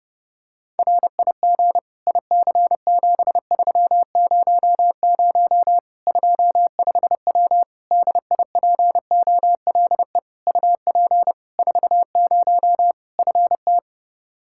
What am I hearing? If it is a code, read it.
Morse code, heard as a keyed tone, about 30 wpm, 700 Hz. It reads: RIG IC7300 25W DIPOLE UP 40 FT
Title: K1USN SST QSO Practice 30wpm